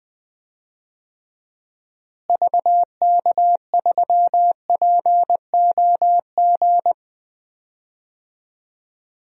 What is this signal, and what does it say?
Morse code, heard as a keyed tone, about 20 wpm, 700 Hz. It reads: VK3POG
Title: DXCC-100 Call Signs - 6 Characters - Repeated 2x - 20wpm